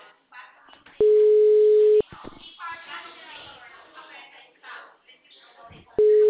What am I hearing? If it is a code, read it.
Ringback tone (Europe, 425 Hz), with speech.